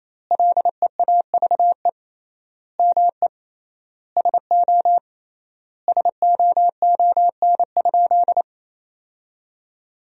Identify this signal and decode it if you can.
Morse code, heard as a keyed tone, about 28 wpm, 700 Hz. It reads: LEAVE ME SO SOON?
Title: 3x Word-Spacing - Sentences from Top 400 Words - 28wpm